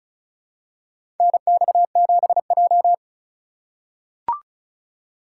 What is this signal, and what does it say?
Morse code, heard as a keyed tone, about 36 wpm, 700 Hz. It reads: NX7J
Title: USA Call Signs - 4 Characters - 4x Character Spacing - 35wpm